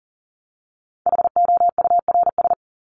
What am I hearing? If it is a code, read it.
Morse code, heard as a keyed tone, about 40 wpm, 700 Hz. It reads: HOURS
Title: ES:S Suffix - Top 100 Words with Base Words 40wpm